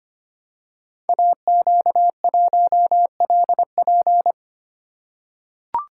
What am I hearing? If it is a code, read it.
Morse code, heard as a keyed tone, about 25 wpm, 700 Hz. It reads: AQ1LP
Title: Code Groups - 5 Letter:Numbers - 25wpm